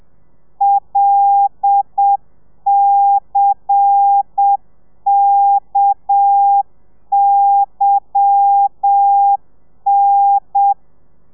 Morse code, heard as a keyed tone, about 7 wpm, 800 Hz. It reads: LCKYN
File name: telegrafia / i2tfj / Lez10 30